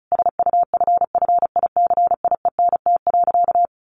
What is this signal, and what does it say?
Morse code, heard as a keyed tone, about 34 wpm, 700 Hz. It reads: SUFFICIENT.
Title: Sentences from Top 1400 Words 35wpm